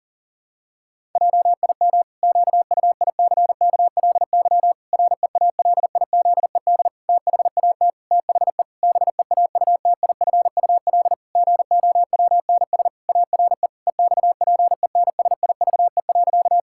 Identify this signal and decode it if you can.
Morse code, heard as a keyed tone, about 40 wpm, 700 Hz. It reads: JIM QUICKLY REALIZED THAT THE BEAUTIFUL GOWNS ARE EXPENSIVE.
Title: Pangram Challenge v2 - 40wpm